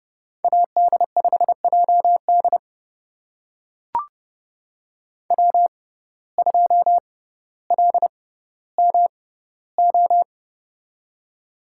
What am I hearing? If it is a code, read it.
Morse code, heard as a keyed tone, about 30 wpm, spaced as for 15 wpm, 700 Hz. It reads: AD5JD W2LMO
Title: USA Call Signs - 5 Characters - 6x Character Spacing - 30wpm